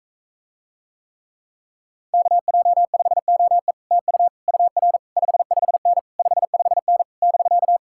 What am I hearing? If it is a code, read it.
Morse code, heard as a keyed tone, about 45 wpm, 700 Hz. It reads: KJ5OE TU UR 55N 55N <BK>